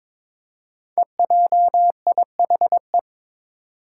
Morse code, heard as a keyed tone, about 21 wpm, 700 Hz. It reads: EJIHE
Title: Five Challenging Letter-Number Combinations 22wpm